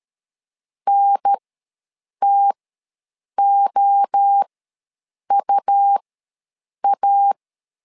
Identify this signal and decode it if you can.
Morse code, heard as a keyed tone, about 13 wpm, spaced as for 7 wpm, 800 Hz. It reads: NTOUA
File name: LEZIONI / 3 MWRO / 3 MWRO ws 12wpm 1914